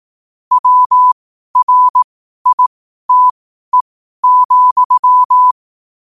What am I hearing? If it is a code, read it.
Morse code, heard as a keyed tone, about 18 wpm, spaced as for 12 wpm, 1000 Hz. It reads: WRITE,
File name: 12wpm